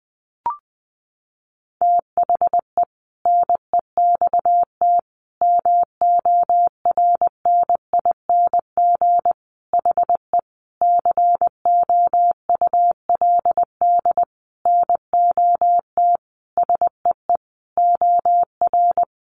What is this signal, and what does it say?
Morse code, heard as a keyed tone, about 20 wpm, 700 Hz. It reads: THE NEXT MORNING HE COULD NOT SEE OR
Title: Sentences from Top 600 Words 20wpm